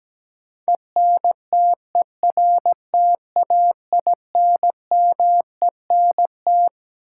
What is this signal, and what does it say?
Morse code, heard as a keyed tone, about 17 wpm, 700 Hz. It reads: ENTERTAINMENT
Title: Sentences from Top 1000 Words - Review Part 6 - New Word Drill - 17wpm